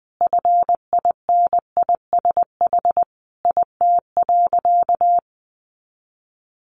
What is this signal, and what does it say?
Morse code, heard as a keyed tone, about 20 wpm, 700 Hz. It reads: FINISH IT.
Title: Sentences from Top 1200 Words 20wpm